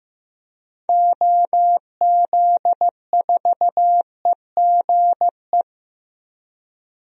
Morse code, heard as a keyed tone, about 15 wpm, 700 Hz. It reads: OZ4EGE
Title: DXCC-100 Call Signs - 6 Characters - Repeated 3x - 15wpm